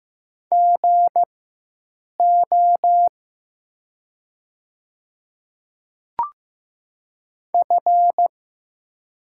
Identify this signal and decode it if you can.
Morse code, heard as a keyed tone, about 15 wpm, spaced as for 7 wpm, 700 Hz. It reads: GO F
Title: 4x Character Spacing - Top 100 Words - 15wpm